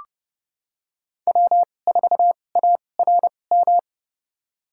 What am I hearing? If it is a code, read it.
Morse code, heard as a keyed tone, about 30 wpm, spaced as for 22 wpm, 700 Hz. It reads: W4ARM
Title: USA Call Signs - 5 Characters - 2x Character Spacing - 30wpm